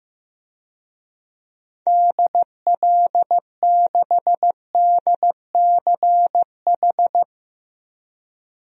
Morse code, heard as a keyed tone, about 16 wpm, 700 Hz. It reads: DL6DCH
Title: DXCC-100 Call Signs - 6 Characters - Repeated 3x - 15wpm